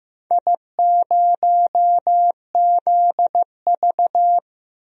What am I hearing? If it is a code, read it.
Morse code, heard as a keyed tone, about 15 wpm, 700 Hz. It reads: I0ZV
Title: DXCC-100 Call Signs - 4 Characters - 15wpm